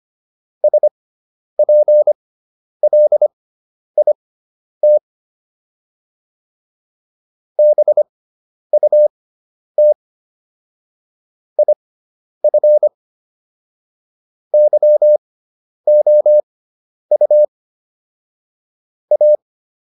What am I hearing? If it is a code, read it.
Morse code, heard as a keyed tone, about 25 wpm, spaced as for 10 wpm, 600 Hz. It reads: SPLIT BUT IF YOU A